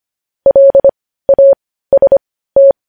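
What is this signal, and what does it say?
Morse code, heard as a keyed tone, about 25 wpm, spaced as for 15 wpm, 550 Hz. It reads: LAST
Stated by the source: Morse news headlines practice at 15.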